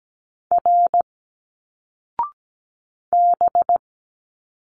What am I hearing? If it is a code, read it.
Morse code, heard as a keyed tone, about 17 wpm, 700 Hz. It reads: R B
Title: Single Letters 17wpm